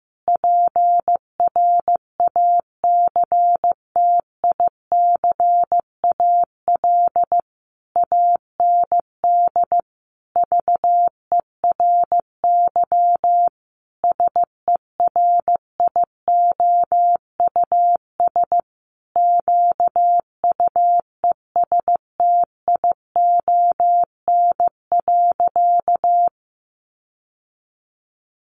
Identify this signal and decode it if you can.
Morse code, heard as a keyed tone, about 15 wpm, 700 Hz. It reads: PRACTICAL AND VERY SERIOUS QUESTION.
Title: Sentences from Top 1400 Words 15wpm